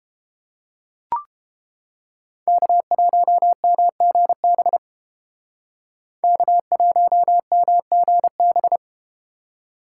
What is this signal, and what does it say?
Morse code, heard as a keyed tone, about 33 wpm, 700 Hz. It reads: K1MGB K1MGB